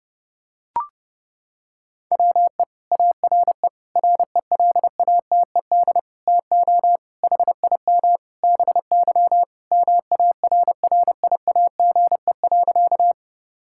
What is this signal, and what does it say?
Morse code, heard as a keyed tone, about 32 wpm, 700 Hz. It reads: WE ARE RELATED TO HIM BY MARRIAGE.
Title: Sentences from Top 1000 Words - Review Part 5 - 30wpm